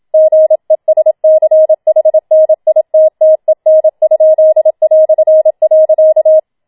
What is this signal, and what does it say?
Morse code, heard as a keyed tone, about 27 wpm, 630 Hz. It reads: GESCHNITTEN?".